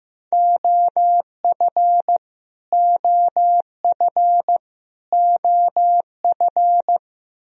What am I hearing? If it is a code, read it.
Morse code, heard as a keyed tone, about 15 wpm, 700 Hz. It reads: OF OF OF
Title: Top 100 Words Repeated 15wpm